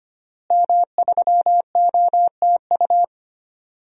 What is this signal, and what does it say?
Morse code, heard as a keyed tone, about 25 wpm, 700 Hz. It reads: M3OTU
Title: DXCC-100 Call Signs - 5 Characters - Repeated 3x - 25wpm